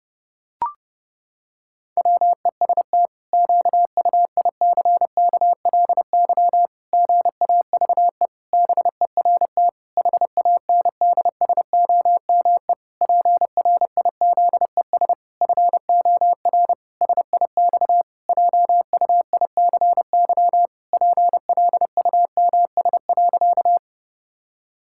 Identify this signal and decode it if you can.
Morse code, heard as a keyed tone, about 30 wpm, 700 Hz. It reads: WEST QUICKLY GAVE BERT HANDSOME PRIZES FOR SIX JUICY PLUMS.
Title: Pangram Challenge v2 - 30wpm